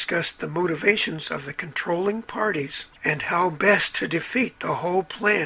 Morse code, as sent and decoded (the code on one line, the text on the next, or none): none